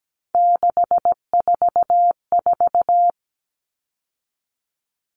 Morse code, heard as a keyed tone, about 17 wpm, 700 Hz. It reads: 644
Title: Numbers - 3-Digit 17wpm